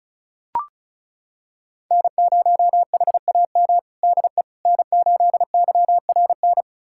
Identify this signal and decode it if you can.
Morse code, heard as a keyed tone, about 34 wpm, 700 Hz. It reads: N0HAM DE N8YRN
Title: USA Callsign DE Callsign - 5 Characters - 35wpm